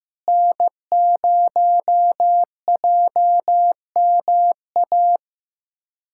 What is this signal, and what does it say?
Morse code, heard as a keyed tone, about 15 wpm, 700 Hz. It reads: N0JMA
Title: USA Callsign DE Callsign 15wpm